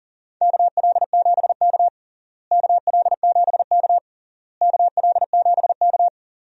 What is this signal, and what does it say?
Morse code, heard as a keyed tone, about 40 wpm, 700 Hz. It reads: KL7K KL7K KL7K
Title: DXCC-100 Call Signs - 4 Characters - Repeated 3x - 40wpm